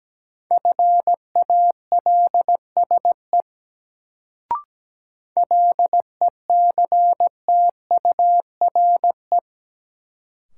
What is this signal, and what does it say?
Morse code, heard as a keyed tone, about 17 wpm, 700 Hz. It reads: FALSE LECTURE
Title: Sentences from Top 1600 Words - New Word Drill - 17wpm